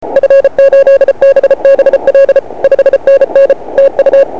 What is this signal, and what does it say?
Morse code, heard as a keyed tone, about 33 wpm, 550 Hz. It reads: F8BBL 5NN TU
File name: dxrecords / 4o3t 160cw